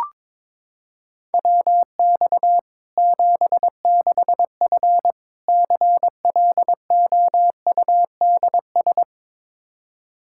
Morse code, heard as a keyed tone, about 22 wpm, 700 Hz. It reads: WX 76F CLOUDS
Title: Practice QSO - Exchange 2 - Info - 22wpm